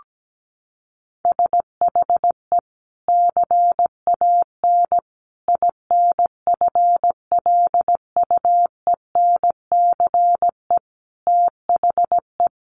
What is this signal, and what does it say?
Morse code, heard as a keyed tone, about 17 wpm, 700 Hz. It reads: SHE CAN INFLUENCE THE
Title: Sentences from Top 1100 Words 17wpm